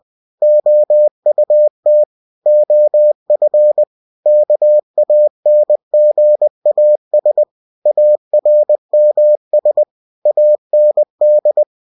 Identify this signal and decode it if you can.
Morse code, heard as a keyed tone, about 20 wpm, 600 Hz. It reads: OUT OF KANGAS ARMS AND